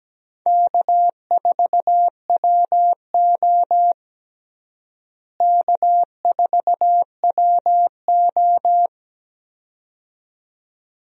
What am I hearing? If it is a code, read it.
Morse code, heard as a keyed tone, about 17 wpm, 700 Hz. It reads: K4WO K4WO
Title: USA Call Signs - 4 Characters - Repeated 2x - 17wpm